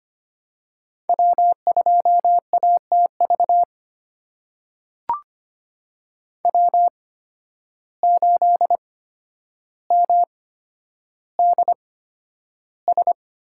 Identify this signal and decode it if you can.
Morse code, heard as a keyed tone, about 25 wpm, spaced as for 11 wpm, 700 Hz. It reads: W2ATV W8MDS
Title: USA Call Signs - 8x Character Spacing - 25wpm